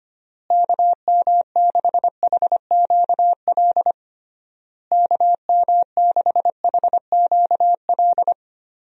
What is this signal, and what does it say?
Morse code, heard as a keyed tone, about 25 wpm, 700 Hz. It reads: KM6HQL KM6HQL
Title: USA Call Signs - 6 Characters - Repeated 3x - 25wpm